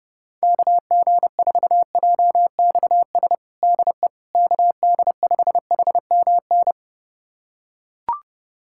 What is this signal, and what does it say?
Morse code, heard as a keyed tone, about 32 wpm, 700 Hz. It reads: KG4JXS DE KD5HMN
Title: USA Callsign DE Callsign - 6 Characters - 30wpm